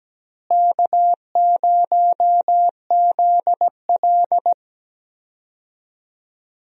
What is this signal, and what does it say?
Morse code, heard as a keyed tone, about 17 wpm, 700 Hz. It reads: K0ZL
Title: USA Call Signs - 4 Characters - Repeated 2x - 17wpm